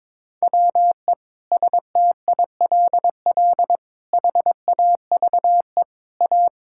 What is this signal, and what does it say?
Morse code, heard as a keyed tone, about 22 wpm, 700 Hz. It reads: WE STILL HAVE A
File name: Sentences with 7-Word Phrases - 22wpm